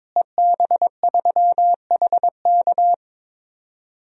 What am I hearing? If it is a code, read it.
Morse code, heard as a keyed tone, about 22 wpm, 700 Hz. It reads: EB3HK
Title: DXCC-100 Call Signs - 5 Characters - Repeated 2x - 22wpm